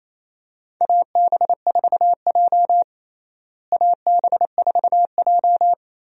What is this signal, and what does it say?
Morse code, heard as a keyed tone, about 28 wpm, 700 Hz. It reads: AB4J AB4J
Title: USA Call Signs - 4 Characters - Repeated 3x - 28wpm